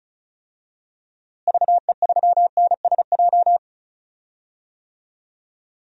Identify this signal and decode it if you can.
Morse code, heard as a keyed tone, about 36 wpm, 700 Hz. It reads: VE3NSJ
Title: DXCC-100 Call Signs - 6 Characters - Repeated 3x - 35wpm